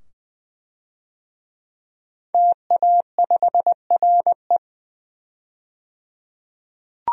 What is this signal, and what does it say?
Morse code, heard as a keyed tone, about 21 wpm, 700 Hz. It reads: TA5RE
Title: DXCC-100 Call Signs - 5 Characters - 20wpm